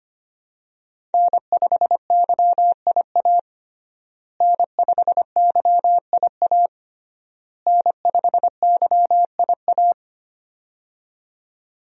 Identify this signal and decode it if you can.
Morse code, heard as a keyed tone, about 25 wpm, 700 Hz. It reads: N5YIA N5YIA N5YIA
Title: USA Call Signs - 5 Characters - Repeated 3x - 25wpm